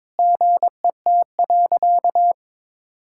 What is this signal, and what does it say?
Morse code, heard as a keyed tone, about 22 wpm, 700 Hz. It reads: GET.
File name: Sentences with 7-Word Phrases - 22wpm